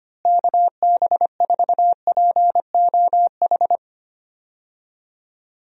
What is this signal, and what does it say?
Morse code, heard as a keyed tone, about 25 wpm, 700 Hz. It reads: KB4POH
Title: USA Call Signs - 6 Characters - Repeated 3x - 25wpm